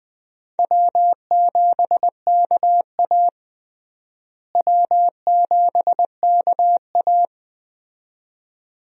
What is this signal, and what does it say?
Morse code, heard as a keyed tone, about 20 wpm, 700 Hz. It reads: W7KA W7KA
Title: USA Call Signs - 4 Characters - Repeated 2x - 20wpm